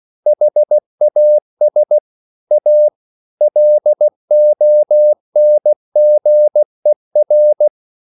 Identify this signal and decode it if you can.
Morse code, heard as a keyed tone, about 16 wpm, 600 Hz. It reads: HAS A LONGER
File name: CW / Practice / EnglishQuotations0216